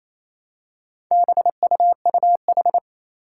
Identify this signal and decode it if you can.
Morse code, heard as a keyed tone, about 28 wpm, 700 Hz. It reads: BUUH
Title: Four Challenging Letter-Number Combinations 28wpm